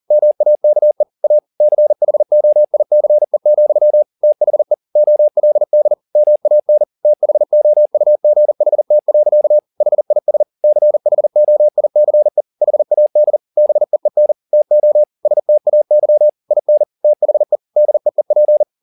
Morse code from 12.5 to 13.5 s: HAD

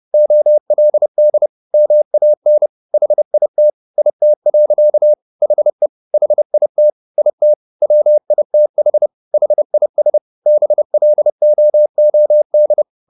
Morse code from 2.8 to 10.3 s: HIT IT. HE HIT IT WITH HIS

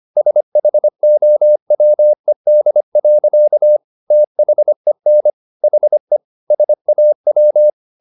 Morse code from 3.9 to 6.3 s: THEN HE

